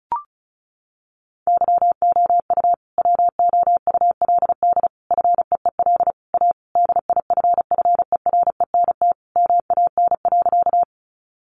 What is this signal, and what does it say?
Morse code, heard as a keyed tone, about 34 wpm, 700 Hz. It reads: YOU WOULD FEEL A DIFFERENT MAN.
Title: Sentences from Top 600 Words 35wpm